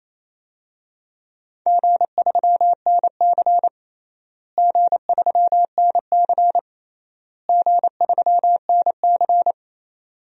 Morse code, heard as a keyed tone, about 28 wpm, 700 Hz. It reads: G3NC G3NC G3NC
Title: DXCC-100 Call Signs - 4 Characters - Repeated 3x - 28wpm